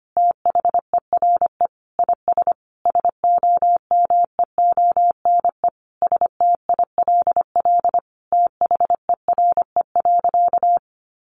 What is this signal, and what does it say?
Morse code, heard as a keyed tone, about 25 wpm, 700 Hz. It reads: THERE IS SOMEONE STILL THERE.
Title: Sentences from Top 1500 Words 25wpm